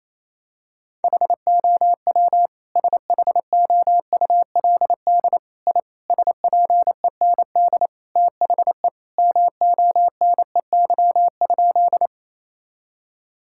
Morse code, heard as a keyed tone, about 27 wpm, 700 Hz. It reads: HOW SHOULD I SPEND THE MONEY?
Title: Sentences from Top 1000 Words - Review Part 5 - 28wpm